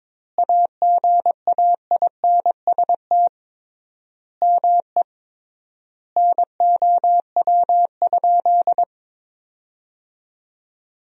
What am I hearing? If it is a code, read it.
Morse code, heard as a keyed tone, about 22 wpm, 700 Hz. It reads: AGAINST ME NOW?
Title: 3x Word-Spacing - Sentences from Top 400 Words - 22wpm